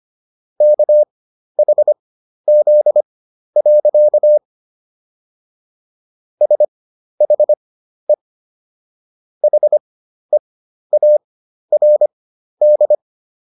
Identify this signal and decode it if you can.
Morse code, heard as a keyed tone, about 25 wpm, spaced as for 12 wpm, 600 Hz. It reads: KHZ. SHE HEARD